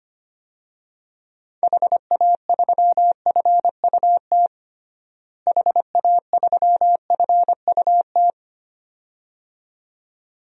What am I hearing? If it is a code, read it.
Morse code, heard as a keyed tone, about 25 wpm, 700 Hz. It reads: HA3FUT HA3FUT
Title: DXCC-100 Call Signs - 6 Characters - Repeated 2x - 25wpm